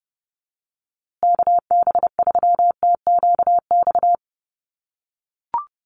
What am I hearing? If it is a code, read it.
Morse code, heard as a keyed tone, about 30 wpm, 700 Hz. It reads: KB3TQX
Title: USA Call Signs Repeated 2x 30wpm